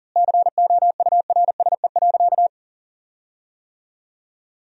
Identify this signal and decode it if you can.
Morse code, heard as a keyed tone, about 37 wpm, 700 Hz. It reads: COURSE.
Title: 4x Word-Spacing - Sentences from Top 500 Words - 40wpm